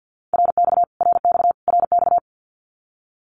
Morse code, heard as a keyed tone, about 50 wpm, 700 Hz. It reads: RX RX RX
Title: Abbreviations Repeated 3x - 50wpm